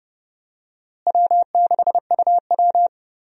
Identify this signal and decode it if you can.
Morse code, heard as a keyed tone, about 29 wpm, 700 Hz. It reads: W6UW
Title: USA Call Signs - 4 Characters - Repeated 2x - 30wpm